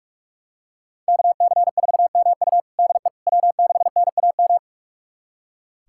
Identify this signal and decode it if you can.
Morse code, heard as a keyed tone, about 45 wpm, 700 Hz. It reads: KC4MU DE W6NAM
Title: USA Callsign DE Callsign - 5 Characters - 45wpm